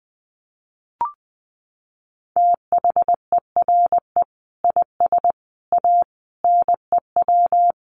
Morse code, heard as a keyed tone, about 20 wpm, 700 Hz. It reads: THERE IS A NEW
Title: Sentences from Top 1500 Words 20wpm